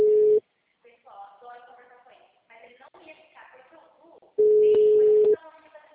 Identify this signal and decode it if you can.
Ringback tone (Europe, 425 Hz), with speech.